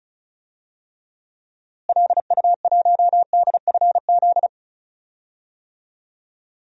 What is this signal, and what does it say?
Morse code, heard as a keyed tone, about 36 wpm, 700 Hz. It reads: LU1DFZ
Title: DXCC-100 Call Signs - 6 Characters - Repeated 2x - 35wpm